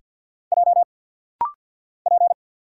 Morse code, heard as a keyed tone, about 50 wpm, 700 Hz. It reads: J P
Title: Single Letter-Number Alt - Rapid-Fire 50wpm